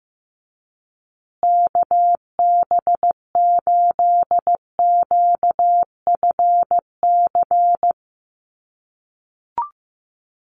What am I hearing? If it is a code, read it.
Morse code, heard as a keyed tone, about 15 wpm, 700 Hz. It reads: KB8QFC
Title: USA Call Signs Repeated 2x 15wpm